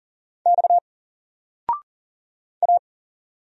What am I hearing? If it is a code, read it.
Morse code, heard as a keyed tone, about 40 wpm, 700 Hz. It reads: X A
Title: Single Letter-Number 40wpm